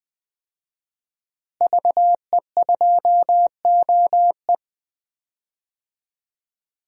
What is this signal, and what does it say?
Morse code, heard as a keyed tone, about 20 wpm, 700 Hz. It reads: VE2OE
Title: DXCC-100 Call Signs - 5 Characters - Repeated 2x - 20wpm